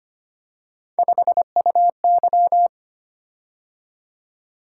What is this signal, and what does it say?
Morse code, heard as a keyed tone, about 25 wpm, 700 Hz. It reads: 5UY
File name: Three Letter-Number Combinations - 25wpm